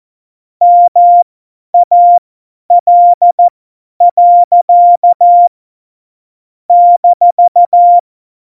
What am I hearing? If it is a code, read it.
Morse code, heard as a keyed tone, about 14 wpm, spaced as for 10 wpm, 700 Hz. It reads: MAL. -
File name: Morsekurs / Witze / Witz 23. Januar 2018, 50 F70